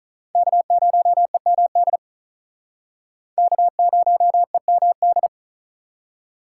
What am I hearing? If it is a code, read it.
Morse code, heard as a keyed tone, about 48 wpm, 700 Hz. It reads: K0EMD K0EMD